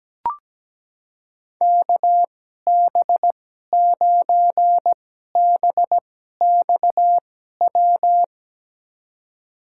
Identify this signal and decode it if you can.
Morse code, heard as a keyed tone, about 17 wpm, spaced as for 12 wpm, 700 Hz. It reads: KB9BXW
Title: USA Call Signs - 6 Characters - 2x Character Spacing - 17wpm